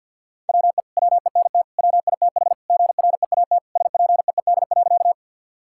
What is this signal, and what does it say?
Morse code, heard as a keyed tone, about 50 wpm, 700 Hz. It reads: WE WENT WITH GREAT SPEED.
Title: Sentences from Top 1000 Words - Review Part 5 - 50wpm